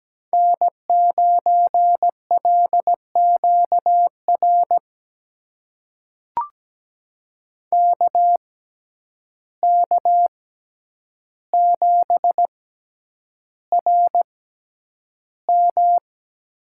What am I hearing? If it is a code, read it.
Morse code, heard as a keyed tone, about 17 wpm, spaced as for 9 wpm, 700 Hz. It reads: N9LQR KK7RM
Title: USA Call Signs - 5 Characters - 6x Character Spacing - 17wpm